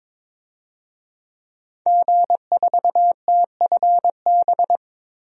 Morse code, heard as a keyed tone, about 22 wpm, 700 Hz. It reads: G4TFB
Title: DXCC-100 Call Signs - 5 Characters - Repeated 3x - 22wpm